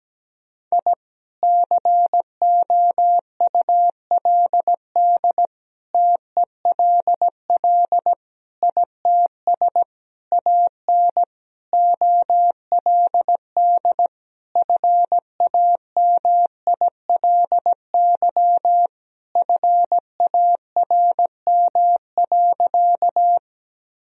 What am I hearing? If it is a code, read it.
Morse code, heard as a keyed tone, about 17 wpm, 700 Hz. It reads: I COULD TELL ITS AN OLD FAMILY FARM.
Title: Sentences from Top 1000 Words - Review Part 2 - 17wpm